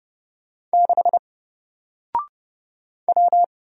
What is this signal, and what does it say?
Morse code, heard as a keyed tone, about 30 wpm, 700 Hz. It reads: 6 W
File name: Single-Letter-Number-30wpm